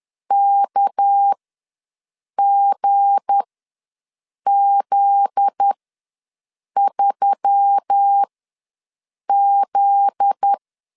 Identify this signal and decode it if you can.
Morse code, heard as a keyed tone, about 10 wpm, spaced as for 6 wpm, 800 Hz. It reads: KGZ3Z
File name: LEZIONI / 7 GF9Z / 7 GF9Z ws 10wpm 3919